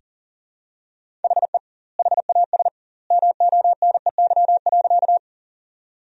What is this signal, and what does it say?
Morse code, heard as a keyed tone, about 40 wpm, 700 Hz. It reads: HE HAS MONEY.
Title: 2x Word-Spacing - Sentences from Top 400 Words - 40wpm